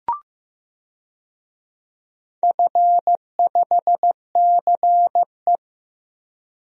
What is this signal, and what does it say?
Morse code, heard as a keyed tone, about 15 wpm, 700 Hz. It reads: F5CE
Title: DXCC-100 Call Signs - 4 Characters - 15wpm